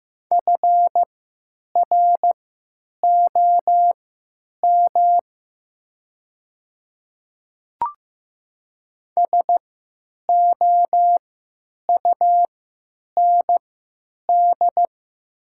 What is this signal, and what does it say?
Morse code, heard as a keyed tone, about 15 wpm, spaced as for 9 wpm, 700 Hz. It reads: FROM SOUND
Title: 3x Character Spacing - Top 100 Words - 15wpm